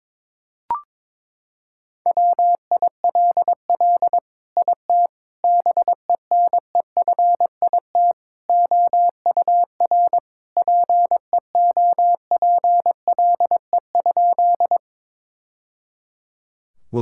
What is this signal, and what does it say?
Morse code, heard as a keyed tone, about 22 wpm, 700 Hz. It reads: WILL IT BENEFIT OUR PEOPLE?
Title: Sentences from Top 1000 Words - Review Part 6 - 22wpm